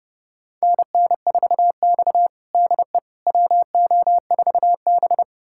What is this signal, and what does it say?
Morse code, heard as a keyed tone, about 30 wpm, 700 Hz. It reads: NN4X DE WO4B
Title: USA Callsign DE Callsign - 4 Characters - 30wpm